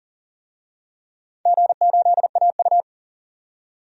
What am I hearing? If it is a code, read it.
Morse code, heard as a keyed tone, about 40 wpm, 700 Hz. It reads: G8AU
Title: DXCC-100 Call Signs - 4 Characters - 40wpm